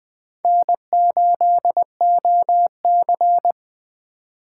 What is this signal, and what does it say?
Morse code, heard as a keyed tone, about 20 wpm, 700 Hz. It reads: N8OC